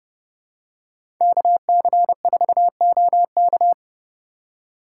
Morse code, heard as a keyed tone, about 30 wpm, 700 Hz. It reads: KC4OK
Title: USA Call Signs - 5 Characters - 3x Character Spacing - 30wpm